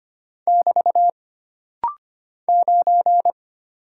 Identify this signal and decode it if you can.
Morse code, heard as a keyed tone, about 25 wpm, 700 Hz. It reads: <BT> 9
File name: Single Character (Letter:Number:Punctuation:Prosign) - Rapid-Fire - 25wpm